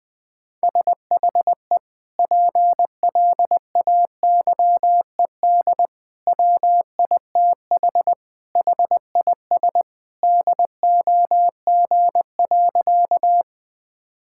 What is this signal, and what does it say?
Morse code, heard as a keyed tone, about 21 wpm, 700 Hz. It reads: SHE PLAYED WITH HIS DOG.
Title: Sentences with 2-Word Phrases - 20wpm